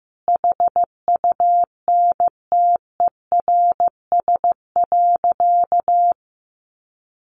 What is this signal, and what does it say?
Morse code, heard as a keyed tone, about 15 wpm, 700 Hz. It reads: HUNTERS.
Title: A Sound Of Thunder 15wpm